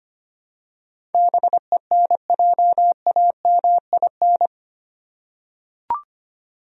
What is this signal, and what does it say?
Morse code, heard as a keyed tone, about 25 wpm, 700 Hz. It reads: BENJAMIN
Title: USA Common Names - 25wpm